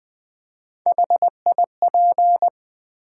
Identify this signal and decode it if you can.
Morse code, heard as a keyed tone, about 20 wpm, 700 Hz. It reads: HIP
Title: Build A Word - 1 Letter At A Time - 3 to 5-Letter Words - 20wpm